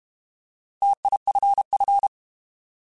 Morse code, heard as a keyed tone, about 32 wpm, 800 Hz. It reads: TIFF